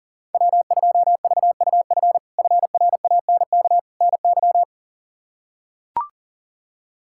Morse code, heard as a keyed tone, about 40 wpm, 700 Hz. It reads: W2VUF FRANK NY
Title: K1USN SST QSO Practice - 40wpm